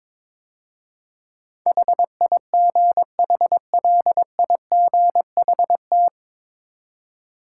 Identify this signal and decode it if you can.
Morse code, heard as a keyed tone, about 22 wpm, 700 Hz. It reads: HIGHLIGHT
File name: 9 Letter Words - First 3 Letters Given - 22wpm